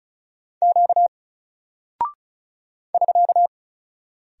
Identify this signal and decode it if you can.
Morse code, heard as a keyed tone, about 36 wpm, 700 Hz. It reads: Q <SK>
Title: Single Character (Letter:Number:Punctuation:Prosign) 35wpm